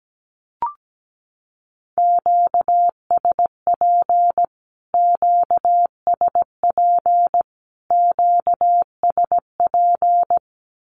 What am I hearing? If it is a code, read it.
Morse code, heard as a keyed tone, about 17 wpm, 700 Hz. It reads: QSP QSP QSP
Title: Q-Codes Repeated 3x 17wpm